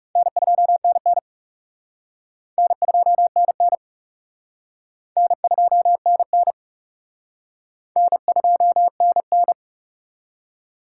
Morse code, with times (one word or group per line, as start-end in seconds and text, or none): 0.2-1.2 N2NN
2.6-3.8 N2NN
5.2-6.5 N2NN
8.0-9.5 N2NN